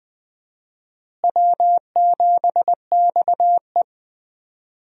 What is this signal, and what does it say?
Morse code, heard as a keyed tone, about 20 wpm, 700 Hz. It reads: W7XE
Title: USA Call Signs - 4 Characters - Repeated 3x - 20wpm